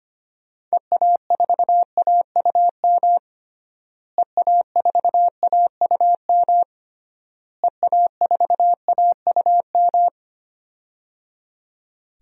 Morse code, heard as a keyed tone, about 25 wpm, 700 Hz. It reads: EA4AUM EA4AUM EA4AUM
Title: DXCC-100 Call Signs - 6 Characters - Repeated 3x - 25wpm